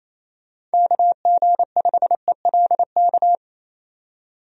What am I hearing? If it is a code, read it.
Morse code, heard as a keyed tone, about 27 wpm, 700 Hz. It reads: KG5ELK
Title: USA Call Signs - 6 Characters - 28wpm